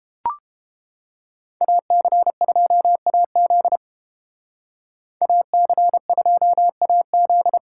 Morse code, heard as a keyed tone, about 29 wpm, 700 Hz. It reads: AC2AZ AC2AZ